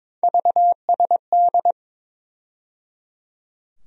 Morse code, heard as a keyed tone, about 21 wpm, 700 Hz. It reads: VSD
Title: USA Call Sign Postfixes - 3 Letters 22wpm